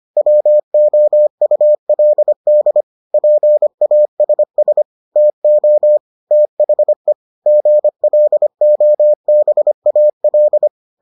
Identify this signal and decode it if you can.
Morse code, heard as a keyed tone, about 25 wpm, 600 Hz. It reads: WOULD PASS TO THE GLOBAL